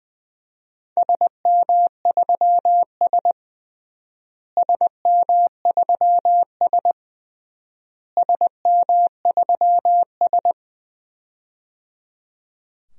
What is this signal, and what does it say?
Morse code, heard as a keyed tone, about 20 wpm, 700 Hz. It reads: SM3S SM3S SM3S
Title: DXCC-100 Call Signs - 4 Characters - Repeated 3x - 20wpm